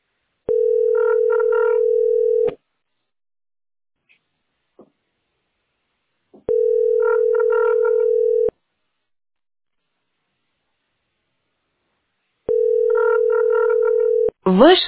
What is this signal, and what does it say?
Ringback tone (North America), 440 + 480 Hz, with speech.
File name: 86e6e8c2-daea-419c-a684